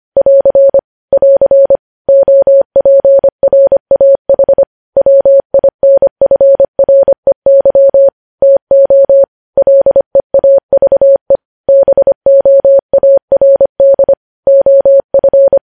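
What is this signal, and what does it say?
Morse code, heard as a keyed tone, about 25 wpm, 550 Hz. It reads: <AR> <AR> OPRAH WINFREY TO LEAVE BOARD OF
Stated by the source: Morse news headlines sent at 25 wpm.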